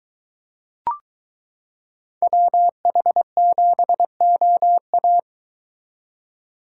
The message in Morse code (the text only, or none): WH7OA